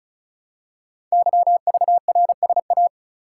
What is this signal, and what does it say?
Morse code, heard as a keyed tone, about 36 wpm, 700 Hz. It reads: YVRSA
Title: Code Groups - 5 Letters - 35wpm